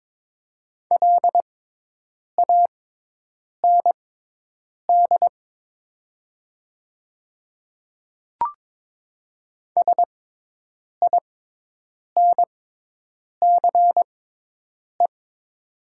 Morse code, heard as a keyed tone, about 22 wpm, spaced as for 8 wpm, 700 Hz. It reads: LAND SINCE
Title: 6x Character Spacing - Top 300 Words - 22wpm